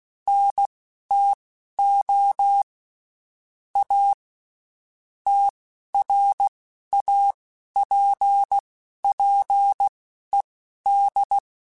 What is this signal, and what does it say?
Morse code, heard as a keyed tone, about 16 wpm, spaced as for 11 wpm, 800 Hz. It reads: NTO A TRAPPED
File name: (spaced) / QSOs Spaced-mp3s / WA7QOX 2 @15 sp